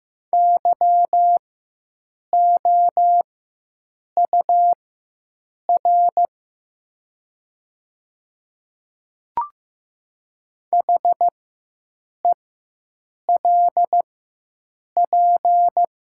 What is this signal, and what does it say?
Morse code, heard as a keyed tone, about 15 wpm, spaced as for 7 wpm, 700 Hz. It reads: YOUR HELP
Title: 4x Character Spacing - Top 500 Words - 15wpm